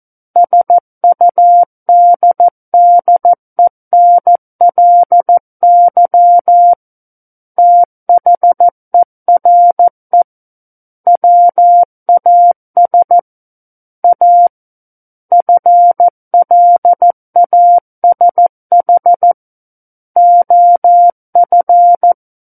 Morse code, heard as a keyed tone, about 14 wpm, 700 Hz. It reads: SUDDENLY THERE WAS A FLASH OF
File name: Book One / 05-Chapter 5 - 14 WPM